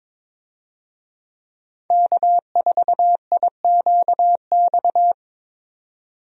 Morse code, heard as a keyed tone, about 22 wpm, 700 Hz. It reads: K4IQX
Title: USA Call Signs - 5 Characters - 22wpm